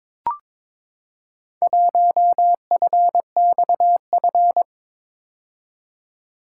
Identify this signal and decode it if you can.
Morse code, heard as a keyed tone, about 22 wpm, 700 Hz. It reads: 1FXF
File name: practice / 22wpm / Four Letter-Number Combinations - 22wpm